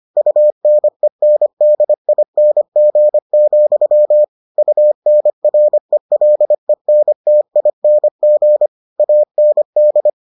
Morse code, heard as a keyed tone, about 25 wpm, 600 Hz. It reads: UNENDING, UNRELENTING AND